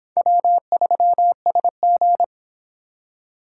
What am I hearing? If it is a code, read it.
Morse code, heard as a keyed tone, about 25 wpm, 700 Hz. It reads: W3SG